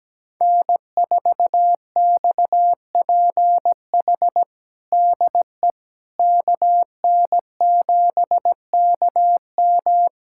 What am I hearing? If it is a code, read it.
Morse code, heard as a keyed tone, about 17 wpm, 700 Hz. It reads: N4XPH DE KN7KM
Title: USA Callsign DE Callsign - 5 Characters - 17wpm